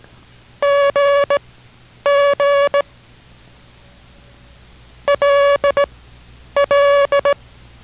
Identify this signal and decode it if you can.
Morse code, heard as a keyed tone, about 18 wpm, spaced as for 9 wpm, 570 Hz. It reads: GG LL